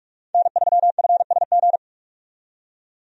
Morse code, heard as a keyed tone, about 46 wpm, 700 Hz. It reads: N3FSG
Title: USA Call Signs - 5 Characters - 45wpm